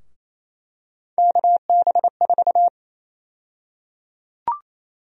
Morse code, heard as a keyed tone, about 28 wpm, 700 Hz. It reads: KB4
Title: USA Call Sign Prefixes - 28wpm